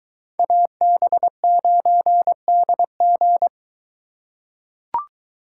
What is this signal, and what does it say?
Morse code, heard as a keyed tone, about 23 wpm, 700 Hz. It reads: AB9DG